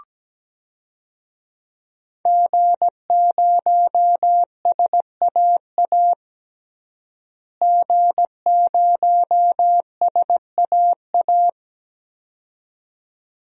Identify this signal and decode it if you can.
Morse code, heard as a keyed tone, about 17 wpm, 700 Hz. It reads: G0SAA G0SAA
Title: DXCC-100 Call Signs - 5 Characters - Repeated 2x - 17wpm